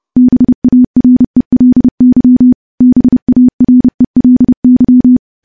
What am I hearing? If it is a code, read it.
Morse code, heard as a keyed tone, about 32 wpm, 250 Hz. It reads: BARELY BARELY
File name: morse / morse-training-30WPM